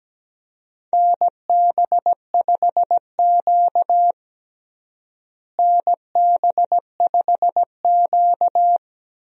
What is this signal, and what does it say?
Morse code, heard as a keyed tone, about 17 wpm, 700 Hz. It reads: NB5Q NB5Q
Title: USA Call Signs - 4 Characters - Repeated 3x - 17wpm